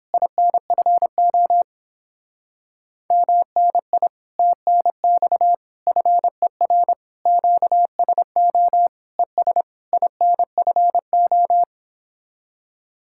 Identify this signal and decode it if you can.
Morse code, heard as a keyed tone, about 31 wpm, 700 Hz. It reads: INFO MNI TNX FER QSO ES INFO